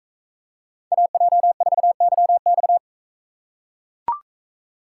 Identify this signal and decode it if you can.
Morse code, heard as a keyed tone, about 42 wpm, 700 Hz. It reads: AJ4YX